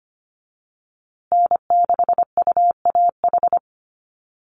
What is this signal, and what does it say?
Morse code, heard as a keyed tone, about 25 wpm, 700 Hz. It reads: N6UAH
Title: USA Call Signs Repeated 2x 25wpm